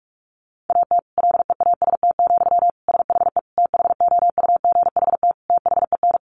Morse code, heard as a keyed tone, about 44 wpm, 700 Hz. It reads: AT LEAST, SHE THOUGHT THEN
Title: Sentences from Top 1400 Words 45wpm